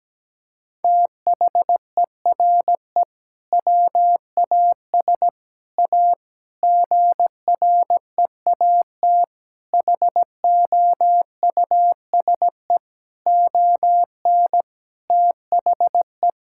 Morse code, heard as a keyed tone, about 17 wpm, 700 Hz. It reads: THERE WAS A GREAT HOUSE ON THE
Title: Sentences from Top 1600 Words 17wpm